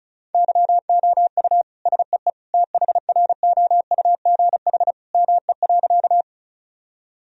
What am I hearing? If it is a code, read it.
Morse code, heard as a keyed tone, about 34 wpm, 700 Hz. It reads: YOU SEE THROUGH ME.
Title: Sentences from Top 1000 Words - Review Part 1 - 35wpm